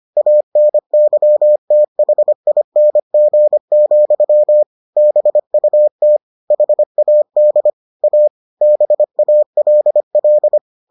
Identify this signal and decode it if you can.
Morse code, heard as a keyed tone, about 25 wpm, 600 Hz. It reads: ANYTHING, BUT HAD A BALL